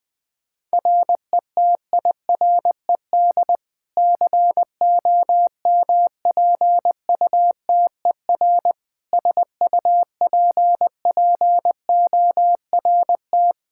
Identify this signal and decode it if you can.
Morse code, heard as a keyed tone, about 21 wpm, 700 Hz. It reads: RETIRED COMPUTER SUPPORT
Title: Practice QSO - Exchange 3 - More Info - 20wpm